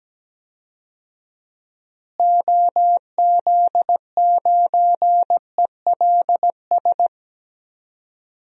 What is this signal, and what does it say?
Morse code, heard as a keyed tone, about 17 wpm, 700 Hz. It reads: OZ9ELS
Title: DXCC-100 Call Signs - 6 Characters - Repeated 2x - 17wpm